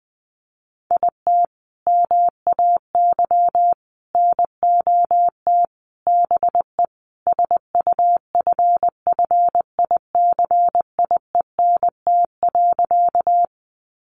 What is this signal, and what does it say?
Morse code, heard as a keyed tone, about 20 wpm, 700 Hz. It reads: IT MAY NOT BE SUFFICIENT.
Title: Sentences from Top 1400 Words 20wpm